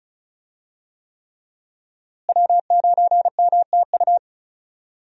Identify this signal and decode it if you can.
Morse code, heard as a keyed tone, about 36 wpm, 700 Hz. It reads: W9MTU
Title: USA Call Signs - 5 Characters - Repeated 3x - 35wpm